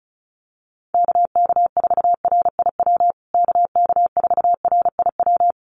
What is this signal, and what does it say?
Morse code, heard as a keyed tone, about 34 wpm, 700 Hz. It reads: KK4RIW KK4RIW
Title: USA Call Signs Repeated 3x 35wpm